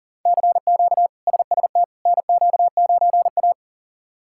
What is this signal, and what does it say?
Morse code, heard as a keyed tone, about 37 wpm, 700 Hz. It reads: CQ SST NQ9A
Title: K1USN SST QSO Practice - 40wpm